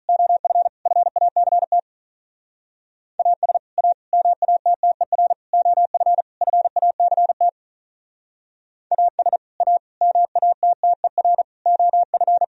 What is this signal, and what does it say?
Morse code, heard as a keyed tone, about 44 wpm, 700 Hz. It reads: OF FACT AS A MATTER OF FACT AS A MATTER OF